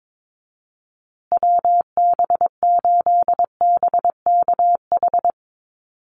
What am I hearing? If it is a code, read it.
Morse code, heard as a keyed tone, about 22 wpm, 700 Hz. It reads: WB8BKH
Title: USA Call Signs Repeated 3x 22wpm